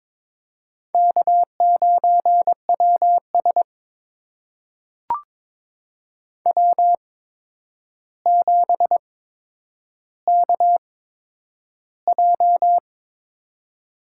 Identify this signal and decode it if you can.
Morse code, heard as a keyed tone, about 23 wpm, spaced as for 10 wpm, 700 Hz. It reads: K9WS W7KJ
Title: USA Call Signs - 4 Characters - 8x Character Spacing - 22wpm